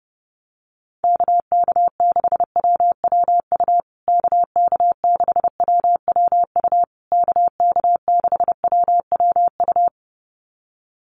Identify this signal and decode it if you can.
Morse code, heard as a keyed tone, about 30 wpm, 700 Hz. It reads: KK6WWU KK6WWU KK6WWU
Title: USA Call Signs Repeated 3x 30wpm